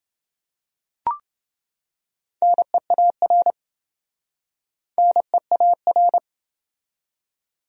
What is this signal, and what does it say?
Morse code, heard as a keyed tone, about 27 wpm, 700 Hz. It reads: NEAR NEAR